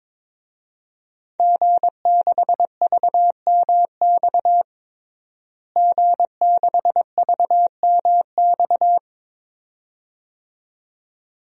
Morse code, heard as a keyed tone, about 22 wpm, 700 Hz. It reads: G6VMX G6VMX
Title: DXCC-100 Call Signs - 5 Characters - Repeated 2x - 22wpm